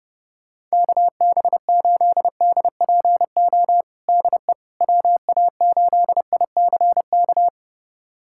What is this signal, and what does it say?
Morse code, heard as a keyed tone, about 32 wpm, 700 Hz. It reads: KB8DPO DE WA8ICK
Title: USA Callsign DE Callsign - 6 Characters - 30wpm